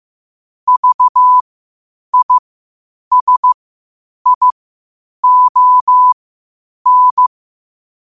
Morse code, heard as a keyed tone, about 15 wpm, spaced as for 8 wpm, 1000 Hz. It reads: VISION
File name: qotdfiles / 8wpm